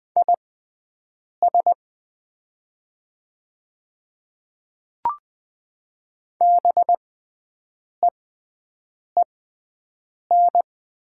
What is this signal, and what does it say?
Morse code, heard as a keyed tone, about 20 wpm, spaced as for 7 wpm, 700 Hz. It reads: IS BEEN
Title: 6x Character Spacing - Top 300 Words - 20wpm